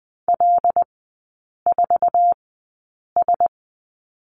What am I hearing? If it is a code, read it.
Morse code, heard as a keyed tone, about 20 wpm, spaced as for 8 wpm, 700 Hz. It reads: L4S
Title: 074 Lesson 13e - Letter L - Three Characters 20wpm